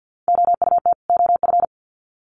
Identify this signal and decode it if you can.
Morse code, heard as a keyed tone, about 50 wpm, 700 Hz. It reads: OUT OF
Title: Sets of 2 Words 50wpm